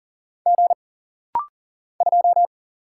Morse code, heard as a keyed tone, about 43 wpm, 700 Hz. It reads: G 2
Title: Single Letter-Number - Rapid-Fire 40wpm